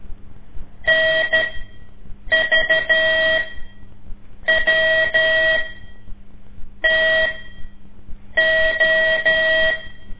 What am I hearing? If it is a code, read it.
Morse code, heard as a keyed tone, about 13 wpm, spaced as for 6 wpm, 640 Hz. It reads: NVWTO
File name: ik1weg2 / CORSI CW / lez5-2